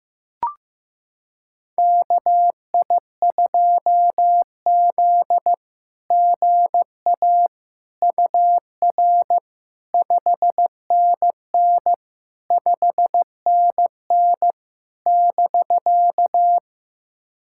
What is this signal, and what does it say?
Morse code, heard as a keyed tone, about 15 wpm, 700 Hz. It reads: KI2Z GA UR 5NN 5NN <BK>
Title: SOTA QSO Practice 15wpm